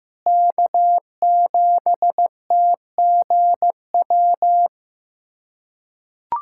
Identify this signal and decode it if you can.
Morse code, heard as a keyed tone, about 15 wpm, 700 Hz. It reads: K7TGW
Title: USA Callsign DE Callsign - 5 Characters - 15wpm